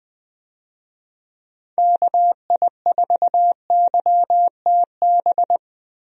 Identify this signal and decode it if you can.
Morse code, heard as a keyed tone, about 21 wpm, 700 Hz. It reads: KI4YTB
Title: USA Call Signs - 6 Characters - Repeated 2x - 20wpm